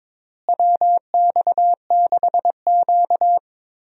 Morse code, heard as a keyed tone, about 22 wpm, 700 Hz. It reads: WX6Q
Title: USA Call Signs - 4 Characters - Repeated 2x - 22wpm